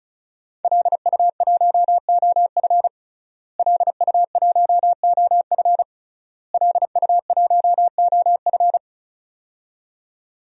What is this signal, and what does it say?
Morse code, heard as a keyed tone, about 35 wpm, 700 Hz. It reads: LU1OF LU1OF LU1OF
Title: DXCC-100 Call Signs - 5 Characters - Repeated 3x - 35wpm